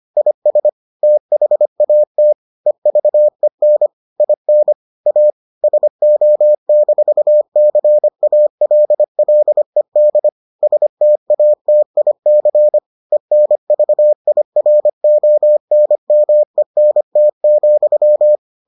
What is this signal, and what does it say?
Morse code, heard as a keyed tone, about 25 wpm, 600 Hz. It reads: IS THAT EVEN IN A SO-CALLED STATIC ENVIRONMENT,